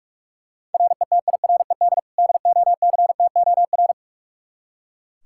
Morse code, heard as a keyed tone, about 41 wpm, 700 Hz. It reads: RETIRED DOCTOR
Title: Practice QSO - Exchange 3 - More Info - 45wpm